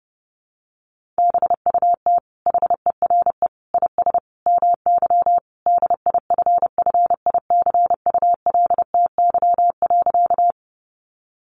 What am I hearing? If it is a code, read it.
Morse code, heard as a keyed tone, about 30 wpm, 700 Hz. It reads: BUT HERE IS MY DIFFICULTY.
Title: Sentences from Top 1500 Words 30wpm